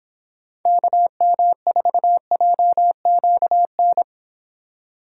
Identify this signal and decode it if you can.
Morse code, heard as a keyed tone, about 29 wpm, 700 Hz. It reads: KM4JQN